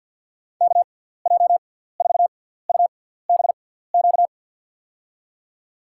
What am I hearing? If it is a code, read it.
Morse code, heard as a keyed tone, about 50 wpm, spaced as for 17 wpm, 700 Hz. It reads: KJ4UBQ
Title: USA Call Signs - 6 Characters - 6x Character Spacing - 50wpm